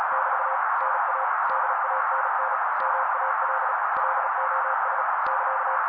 Morse code, heard as a keyed tone, about 36 wpm, 530 Hz. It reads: <BT> NA DANN MAL LOS <KA>
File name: daily / 35wpm noise